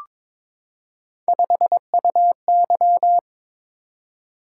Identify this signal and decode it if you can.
Morse code, heard as a keyed tone, about 22 wpm, 700 Hz. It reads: 5UY
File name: practice / 22wpm / Three Letter-Number Combinations - 22wpm